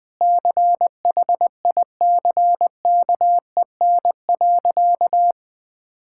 Morse code, heard as a keyed tone, about 21 wpm, 700 Hz. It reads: CHICKEN.
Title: Sentences from Top 1000 Words - Review Part 5 - 20wpm